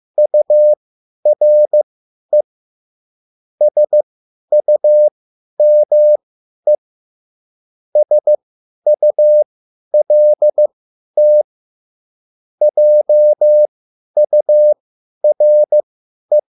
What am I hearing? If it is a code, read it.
Morse code, heard as a keyed tone, about 15 wpm, spaced as for 10 wpm, 600 Hz. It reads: URE SUME SULT JURE